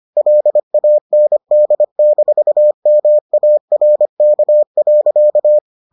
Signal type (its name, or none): Morse code